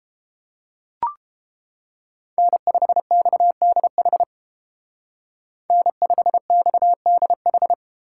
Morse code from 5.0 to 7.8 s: N5XDH

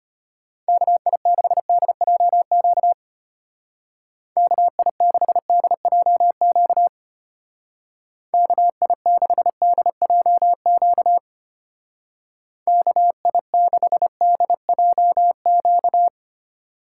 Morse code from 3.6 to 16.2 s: KI6DJQ KI6DJQ KI6DJQ

